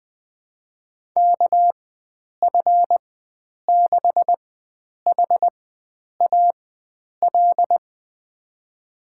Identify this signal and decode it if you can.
Morse code, heard as a keyed tone, about 21 wpm, spaced as for 9 wpm, 700 Hz. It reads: KF6HAL
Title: USA Call Signs - 6 Characters - 4x Character Spacing - 20wpm